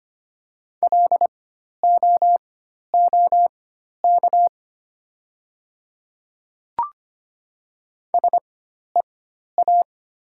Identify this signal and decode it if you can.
Morse code, heard as a keyed tone, about 25 wpm, spaced as for 12 wpm, 700 Hz. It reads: LOOK SEA
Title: 4x Character Spacing - Top 500 Words - 25wpm